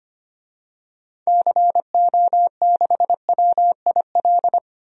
Morse code, heard as a keyed tone, about 25 wpm, 700 Hz. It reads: CO6WIL
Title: DXCC-100 Call Signs - 6 Characters - Repeated 2x - 25wpm